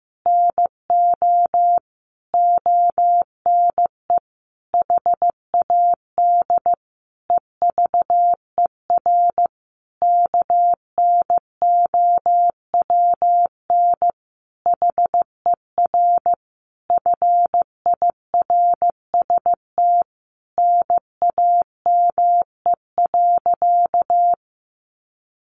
Morse code, heard as a keyed tone, about 15 wpm, 700 Hz. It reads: NO ONE HAD EVER KNOWN HER FIRST NAME.
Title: Sentences from Top 1200 Words 15wpm